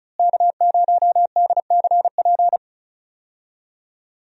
Morse code, heard as a keyed tone, about 36 wpm, 700 Hz. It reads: K0DCP
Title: USA Call Signs - 5 Characters - Repeated 2x - 35wpm